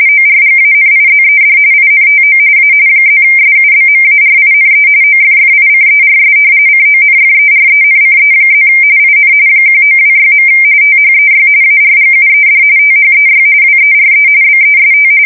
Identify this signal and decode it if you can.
RTTY, about 45.45 baud. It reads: ORGANIZED INTO A +SUBMARINE OFFICERS CONFERENCE+ IN 1926. THEY ARGUED THEY WERE BEST
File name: fsk170 / usnavy4